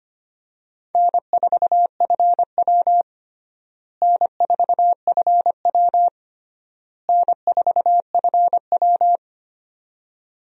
Morse code, heard as a keyed tone, about 25 wpm, 700 Hz. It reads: N4FW N4FW N4FW
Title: USA Call Signs - 4 Characters - Repeated 3x - 25wpm